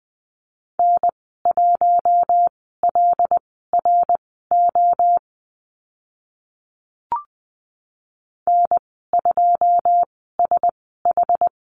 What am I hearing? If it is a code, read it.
Morse code, heard as a keyed tone, about 20 wpm, spaced as for 14 wpm, 700 Hz. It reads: N1LRO N2SH
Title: 082 Lesson 14g - Number 2 - Call Signs 20wpm